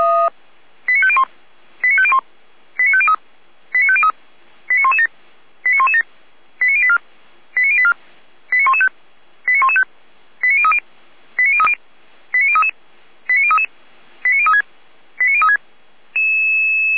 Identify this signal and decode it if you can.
Five-tone selective call (ZVEI), heard as quick runs of five tones, tones of about 70 ms. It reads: 89501 89501 89503 89503 89117 89117 89084 89084 89115 89115 89030 89030 89033 89033 89036 89036